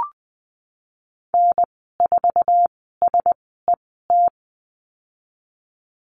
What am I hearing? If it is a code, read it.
Morse code, heard as a keyed tone, about 20 wpm, spaced as for 15 wpm, 700 Hz. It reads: N4SET
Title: 213 Lesson 34g - Number 8 - Call Signs 20wpm